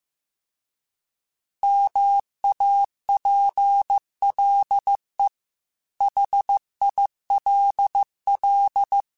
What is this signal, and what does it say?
Morse code, heard as a keyed tone, about 15 wpm, 800 Hz. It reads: MAPLE HILL